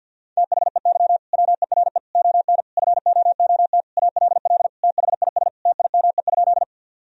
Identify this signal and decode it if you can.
Morse code, heard as a keyed tone, about 50 wpm, 700 Hz. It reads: THEY WERE ON FOOT ALL THIS TIME?
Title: Sentences from Top 1000 Words - Review Part 3 - 50wpm